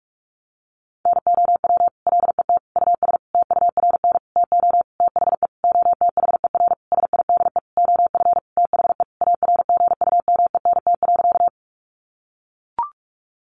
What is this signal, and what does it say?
Morse code, heard as a keyed tone, about 44 wpm, 700 Hz. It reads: NOW LET US TURN TO THE OTHER SIDE OF THE ARGUMENT.
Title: Sentences from Top 1400 Words 45wpm